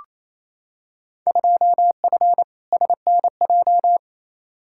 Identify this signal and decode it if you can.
Morse code, heard as a keyed tone, about 28 wpm, 700 Hz. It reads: 2F SNJ
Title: Field Day Exchanges Encore - 28wpm